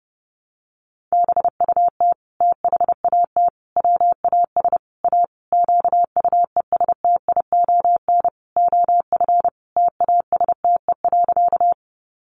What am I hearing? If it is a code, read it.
Morse code, heard as a keyed tone, about 28 wpm, 700 Hz. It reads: BUT THAT WAS A QUESTION OF TASTE.
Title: Sentences from Top 1100 Words 30wpm